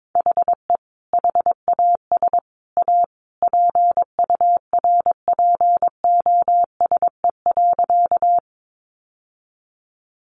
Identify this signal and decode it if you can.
Morse code, heard as a keyed tone, about 22 wpm, 700 Hz. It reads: HE HAS A PURPOSE.
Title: Sentences from Top 800 Words 22wpm